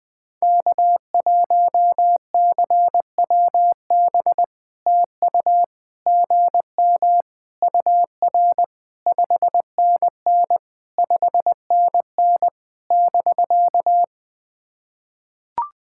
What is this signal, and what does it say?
Morse code, heard as a keyed tone, about 20 wpm, 690 Hz. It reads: K1CWB TU GM UR 5NN 5NN <BK>
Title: POTA QSO Practice - 20wpm